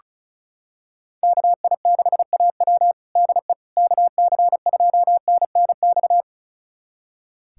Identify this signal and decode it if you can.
Morse code, heard as a keyed tone, about 34 wpm, 700 Hz. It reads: KI6AW DE KC2NNX
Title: USA Callsign DE Callsign 35wpm